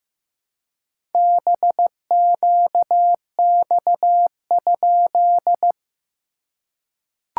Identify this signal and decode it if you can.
Morse code, heard as a keyed tone, about 15 wpm, 700 Hz. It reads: BQX?
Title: SOTA QSO Practice 15wpm